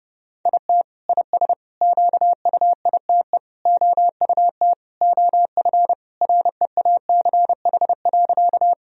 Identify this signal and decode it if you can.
Morse code, heard as a keyed tone, about 30 wpm, 700 Hz. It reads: IT IS QUITE OUT OF REACH.
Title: Sentences from Top 1000 Words - Review Part 2 - 30wpm